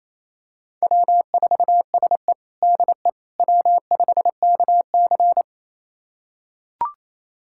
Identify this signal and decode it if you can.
Morse code, heard as a keyed tone, about 28 wpm, 700 Hz. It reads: W4SE DE W5KC
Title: USA Callsign DE Callsign - 4 Characters - 28wpm